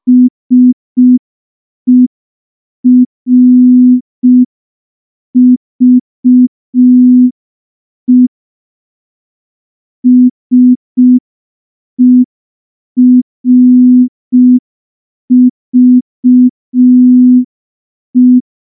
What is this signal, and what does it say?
Morse code, heard as a keyed tone, about 5 wpm, 250 Hz. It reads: SERVE SERVE
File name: morse / morse-training-05WPM